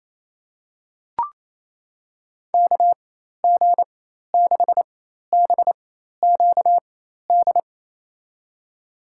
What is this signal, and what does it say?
Morse code, heard as a keyed tone, about 27 wpm, spaced as for 13 wpm, 700 Hz. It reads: KG6BQD
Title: USA Call Signs - 4x Character Spacing - 28wpm